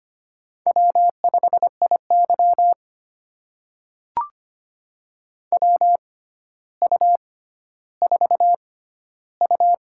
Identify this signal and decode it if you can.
Morse code, heard as a keyed tone, about 25 wpm, spaced as for 13 wpm, 700 Hz. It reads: W5IY WU4U
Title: USA Call Signs - 4 Characters - 6x Character Spacing - 25wpm